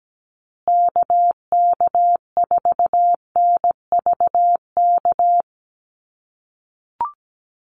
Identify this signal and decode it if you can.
Morse code, heard as a keyed tone, about 17 wpm, 700 Hz. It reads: KK4NVK
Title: USA Call Signs Repeated 2x 17wpm